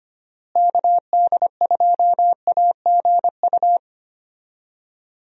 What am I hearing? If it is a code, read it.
Morse code, heard as a keyed tone, about 25 wpm, 700 Hz. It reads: KD2AGU
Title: USA Call Signs - 6 Characters - Repeated 2x - 25wpm